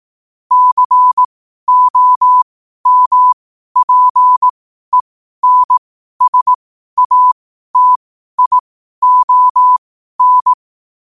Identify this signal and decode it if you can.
Morse code, heard as a keyed tone, about 18 wpm, spaced as for 12 wpm, 1000 Hz. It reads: COMPENSATION
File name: qotdfiles / 12wpm